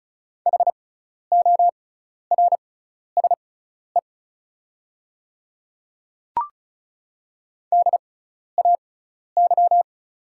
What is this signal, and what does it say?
Morse code, heard as a keyed tone, about 36 wpm, spaced as for 12 wpm, 700 Hz. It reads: HORSE DAY
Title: 6x Character Spacing - Top 500 Words - 35wpm